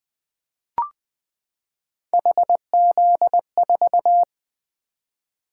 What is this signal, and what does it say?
Morse code, heard as a keyed tone, about 20 wpm, 700 Hz. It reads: HZ4
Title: Three Letter-Number Combinations - 20wpm